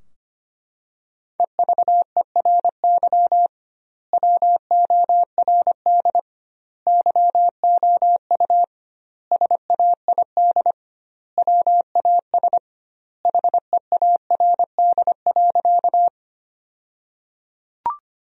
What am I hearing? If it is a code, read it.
Morse code, heard as a keyed tone, about 25 wpm, 700 Hz. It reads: EVERY WORD YOU SAID WAS HEARD.
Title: 2x Word-Spacing - Sentences from Top 400 Words - 25wpm